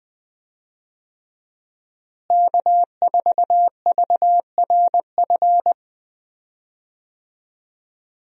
Morse code, heard as a keyed tone, about 20 wpm, 700 Hz. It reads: K4VRF
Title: USA Call Signs - 5 Characters - Repeated 2x - 20wpm